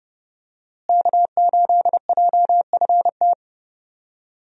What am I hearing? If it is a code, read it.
Morse code, heard as a keyed tone, about 31 wpm, 700 Hz. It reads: K8JFT
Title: USA Call Signs - 5 Characters - Repeated 2x - 30wpm